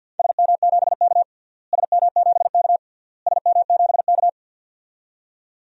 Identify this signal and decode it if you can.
Morse code, heard as a keyed tone, about 47 wpm, 700 Hz. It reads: SM7K SM7K SM7K
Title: DXCC-100 Call Signs - 4 Characters - Repeated 3x - 50wpm